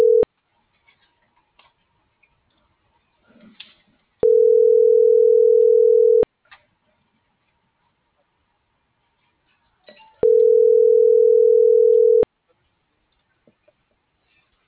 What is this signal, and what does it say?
Ringback tone (North America), 440 + 480 Hz.